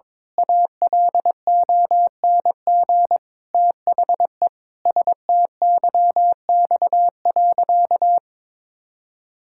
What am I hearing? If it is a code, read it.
Morse code, heard as a keyed tone, about 22 wpm, 700 Hz. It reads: ALONG THE STYX.
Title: Pangram Challenge v2 - 22wpm